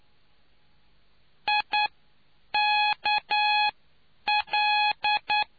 Morse code, heard as a keyed tone, about 9 wpm, spaced as for 8 wpm, 840 Hz. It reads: IKL